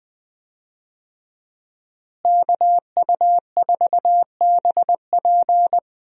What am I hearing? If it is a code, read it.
Morse code, heard as a keyed tone, about 20 wpm, 700 Hz. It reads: KU4BP
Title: USA Call Signs - 5 Characters - Repeated 2x - 20wpm